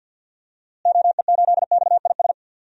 Morse code, heard as a keyed tone, about 46 wpm, 700 Hz. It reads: OE8XIS
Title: DXCC-100 Call Signs - 6 Characters - Repeated 3x - 50wpm